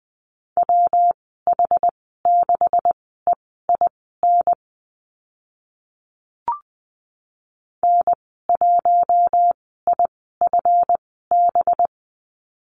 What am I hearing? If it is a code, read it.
Morse code, heard as a keyed tone, about 20 wpm, spaced as for 14 wpm, 700 Hz. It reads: WH6EIN N1IFB
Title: 186 Lesson 30g - Letter b - Call Signs 20wpm